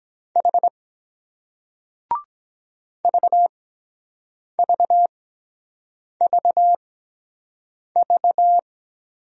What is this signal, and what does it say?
Morse code, heard as a keyed tone, about 22 wpm, spaced as for 7 wpm, 700 Hz. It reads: H VVVV